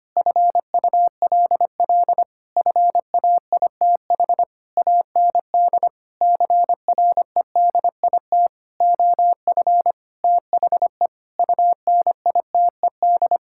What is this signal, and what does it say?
Morse code, heard as a keyed tone, about 25 wpm, 700 Hz. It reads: FULL FAITH AND CREDIT OF THE UNITED
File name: practice / 25wpm / Sentences with 9-Word Phrases - 25wpm